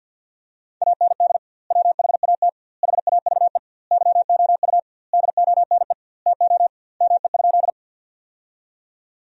Morse code, heard as a keyed tone, about 50 wpm, 700 Hz. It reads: AND WHAT HAVE YOU DONE TO ME?
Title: 2x Word-Spacing - Sentences from Top 500 Words - 50wpm